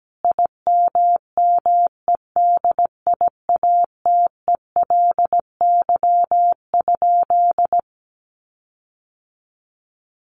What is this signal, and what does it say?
Morse code, heard as a keyed tone, about 17 wpm, 700 Hz. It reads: IMMEDIATELY?
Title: Sentences from Top 1300 Words 17wpm